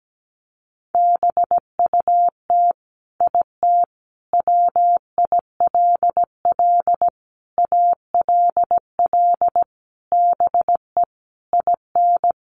Morse code, heard as a keyed tone, about 17 wpm, 700 Hz. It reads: BUT IT WILL ALL BE IN
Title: Sentences from Top 1400 Words 17wpm